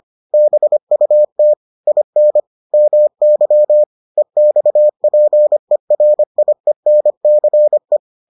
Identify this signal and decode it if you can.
Morse code, heard as a keyed tone, about 25 wpm, 600 Hz. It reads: BUT IN MY EXPERIENCE